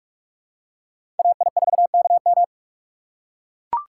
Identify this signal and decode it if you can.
Morse code, heard as a keyed tone, about 41 wpm, 700 Hz. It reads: AI4KM
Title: USA Call Signs - 5 Characters - 3x Character Spacing - 45wpm